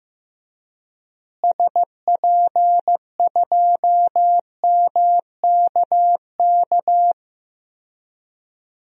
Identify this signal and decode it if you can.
Morse code, heard as a keyed tone, about 15 wpm, 700 Hz. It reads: SP2MKK
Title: DXCC-100 Call Signs - 6 Characters - Repeated 3x - 15wpm